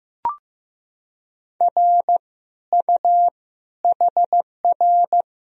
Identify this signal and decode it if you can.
Morse code, heard as a keyed tone, about 15 wpm, 700 Hz. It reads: R U HR
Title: Sentences from Top 400 Words and Abbreviations 15wpm